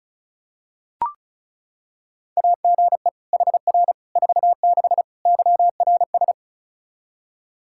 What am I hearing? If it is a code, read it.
Morse code, heard as a keyed tone, about 34 wpm, 700 Hz. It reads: AGE HR 46 YRS
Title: K1USN SST QSO Practice 35wpm